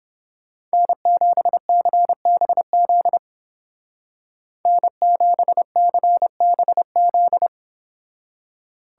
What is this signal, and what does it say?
Morse code, heard as a keyed tone, about 28 wpm, 700 Hz. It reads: N7CBZ N7CBZ